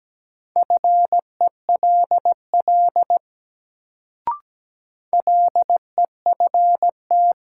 Morse code, heard as a keyed tone, about 17 wpm, 700 Hz. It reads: FELL LEFT
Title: Sets of 5 Words Encore - New Word Drill - 17wpm